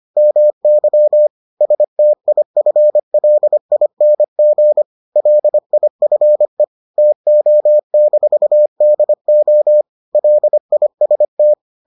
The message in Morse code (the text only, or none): MY STIFLING LIFE TO-DO LIST